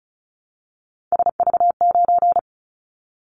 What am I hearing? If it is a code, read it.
Morse code, heard as a keyed tone, about 36 wpm, 700 Hz. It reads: SV9
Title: Callsign Prefixes for Top 100 DXCC - 35wpm